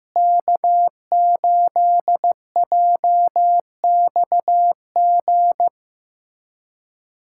Morse code, heard as a keyed tone, about 15 wpm, 700 Hz. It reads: K8JXG
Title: USA Callsign DE Callsign 15wpm